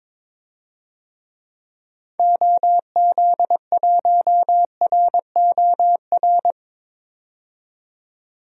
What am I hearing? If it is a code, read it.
Morse code, heard as a keyed tone, about 22 wpm, 700 Hz. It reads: OZ1ROR
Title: DXCC-100 Call Signs - 6 Characters - 22wpm